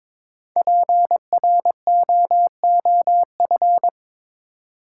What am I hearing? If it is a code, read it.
Morse code, heard as a keyed tone, about 22 wpm, 700 Hz. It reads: PROOF
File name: 5 Letter Words - First 3 Letters Given - 22wpm